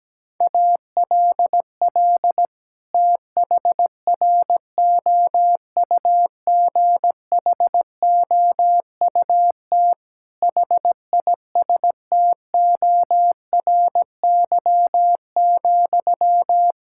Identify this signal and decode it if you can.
Morse code, heard as a keyed tone, about 17 wpm, 700 Hz. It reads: ALL THROUGHOUT HISTORY,
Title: Sentences from Top 1000 Words - Review Part 5 - 17wpm